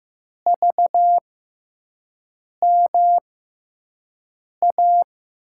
Morse code, heard as a keyed tone, about 15 wpm, spaced as for 5 wpm, 700 Hz. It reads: VMA
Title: USA Call Signs - 6 Characters - 6x Character Spacing - 15wpm